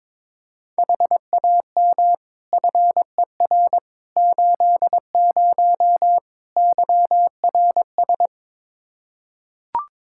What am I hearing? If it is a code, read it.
Morse code, heard as a keyed tone, about 22 wpm, 700 Hz. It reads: HAM FER 80 YRS
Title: Practice QSO - Exchange 3 - More Info - 22wpm